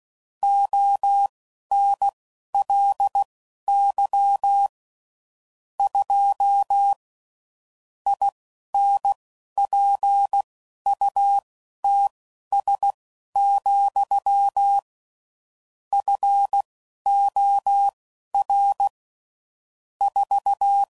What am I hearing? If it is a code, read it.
Morse code, heard as a keyed tone, about 16 wpm, spaced as for 11 wpm, 800 Hz. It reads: ONLY 2 INPUTS, FOR 4
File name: CW Code Groups @10WPM/Technical Subjects